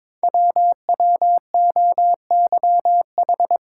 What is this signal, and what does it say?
Morse code, heard as a keyed tone, about 22 wpm, 700 Hz. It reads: WWOYH
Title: Code Groups - 5 Letters - 22wpm